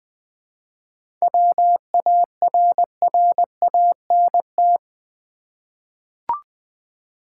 Morse code, heard as a keyed tone, about 20 wpm, 690 Hz. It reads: WARRANT
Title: 7 Letter Words - First 3 Letters Given - 20wpm